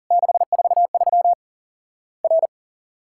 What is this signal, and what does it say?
Morse code, heard as a keyed tone, about 40 wpm, 700 Hz. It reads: 643 R
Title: Worked All Europe DX Contest - QTC Exchange - 40wpm